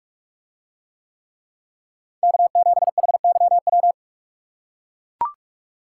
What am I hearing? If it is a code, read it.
Morse code, heard as a keyed tone, about 44 wpm, 700 Hz. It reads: K7HYW